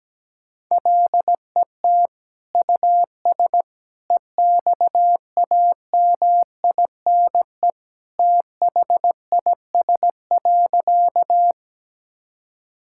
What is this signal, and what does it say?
Morse code, heard as a keyed tone, about 17 wpm, 700 Hz. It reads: LET US EXAMINE THIS.
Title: Sentences from Top 1600 Words 17wpm